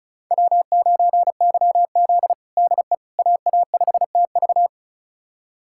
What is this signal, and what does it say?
Morse code, heard as a keyed tone, about 36 wpm, 700 Hz. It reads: W9YZ DE AA5TV
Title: USA Callsign DE Callsign 35wpm